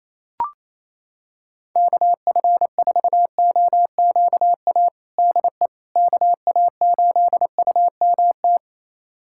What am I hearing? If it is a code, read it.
Morse code, heard as a keyed tone, about 27 wpm, 700 Hz. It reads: KF4OQA DE KA8UMT
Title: USA Callsign DE Callsign - 6 Characters - 28wpm